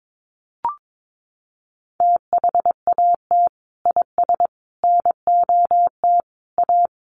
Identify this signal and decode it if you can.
Morse code, heard as a keyed tone, about 22 wpm, 700 Hz. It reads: THAT IS NOT A
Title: Sentences from Top 600 Words 22wpm